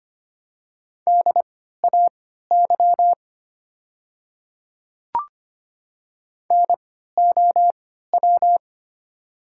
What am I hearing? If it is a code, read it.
Morse code, heard as a keyed tone, about 25 wpm, spaced as for 14 wpm, 690 Hz. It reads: DAY NOW
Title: 3x Character Spacing - Top 200 Words - 25wpm